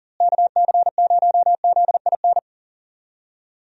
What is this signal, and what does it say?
Morse code, heard as a keyed tone, about 40 wpm, 700 Hz. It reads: KC0ZIN
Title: USA Call Signs - 6 Characters - Repeated 3x - 40wpm